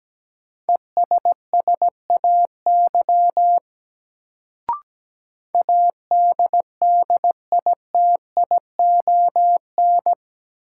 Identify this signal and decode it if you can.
Morse code, heard as a keyed tone, about 17 wpm, 700 Hz. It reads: ESSAY ADDITION
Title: Sentences from Top 1500 Words - New Word Drill Alt - 17wpm